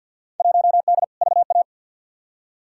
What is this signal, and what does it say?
Morse code, heard as a keyed tone, about 52 wpm, 700 Hz. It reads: 1D VA
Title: Field Day Exchanges - 50wpm